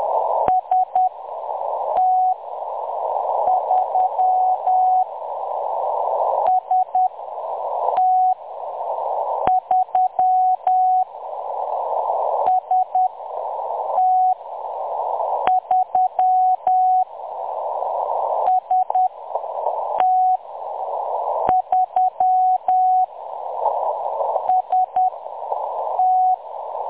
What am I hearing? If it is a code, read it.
Morse code, heard as a keyed tone, about 10 wpm, spaced as for 6 wpm, 740 Hz. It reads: ST3 ST3 ST3 ST3 ST
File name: M23-ST3- -0858utc-10184khz